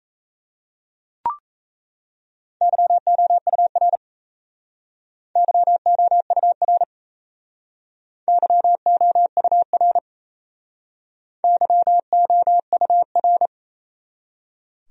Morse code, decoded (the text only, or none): YOUR YOUR YOUR YOUR